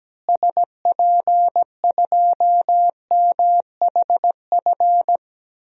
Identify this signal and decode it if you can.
Morse code, heard as a keyed tone, about 17 wpm, 700 Hz. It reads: SP2MHF
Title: DXCC-100 Call Signs - 6 Characters - Repeated 2x - 17wpm